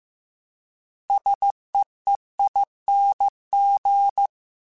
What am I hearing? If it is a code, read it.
Morse code, heard as a keyed tone, about 15 wpm, 800 Hz. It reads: SEEING